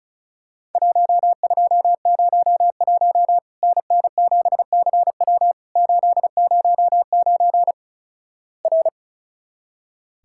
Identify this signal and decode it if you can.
Morse code, heard as a keyed tone, about 36 wpm, 700 Hz. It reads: 1201 NN7CW 809 R
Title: Worked All Europe DX Contest - QTC Exchange - 35wpm